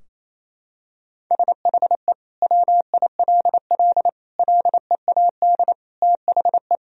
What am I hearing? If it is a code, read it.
Morse code, heard as a keyed tone, about 27 wpm, 700 Hz. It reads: SHE WILL LEAD THE
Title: Sentences from Top 1000 Words - Review Part 2 - 28wpm